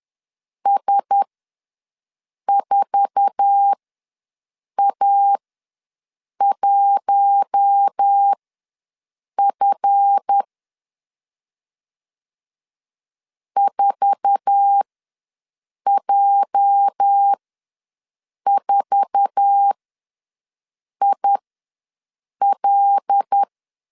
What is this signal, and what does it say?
Morse code, heard as a keyed tone, about 10 wpm, spaced as for 6 wpm, 800 Hz. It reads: S4A1F 4J4IL
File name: LEZIONI / 8 YJ45 / 8 YJ4 ws 10wpm 4020